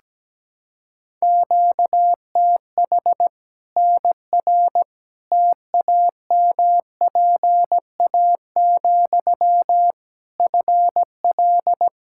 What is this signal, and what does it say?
Morse code, heard as a keyed tone, about 17 wpm, 700 Hz. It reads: QTH NR TAMPA, FL
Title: K1USN SST QSO Practice 17wpm